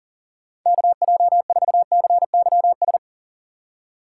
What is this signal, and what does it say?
Morse code, heard as a keyed tone, about 37 wpm, 700 Hz. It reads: KJ4CYS
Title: USA Call Signs - 6 Characters - 4x Character Spacing - 40wpm